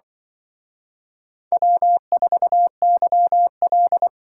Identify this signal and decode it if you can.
Morse code, heard as a keyed tone, about 24 wpm, 700 Hz. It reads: W4YL
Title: Speed Racing - USA 1x2 Call Signs 20wpm